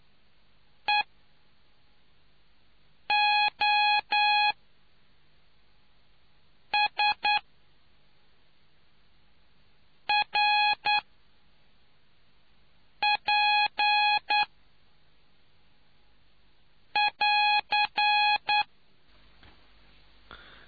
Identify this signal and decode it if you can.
Morse code, heard as a keyed tone, about 9 wpm, spaced as for 3 wpm, 830 Hz. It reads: EOSRP<AR>